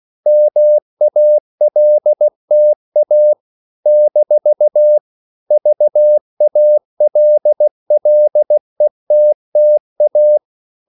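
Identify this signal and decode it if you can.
Morse code, heard as a keyed tone, about 16 wpm, 600 Hz. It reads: MALTA - VALLETTA
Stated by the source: CW practice of national capitals at 16 words per minute, from ON6ZQ.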